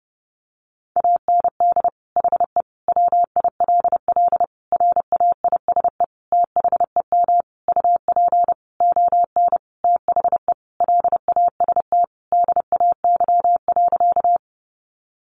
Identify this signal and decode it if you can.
Morse code, heard as a keyed tone, about 30 wpm, 700 Hz. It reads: AND HE WILL RAISE THEM UP ON THE LAST DAY.
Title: Sentences from Top 1400 Words 30wpm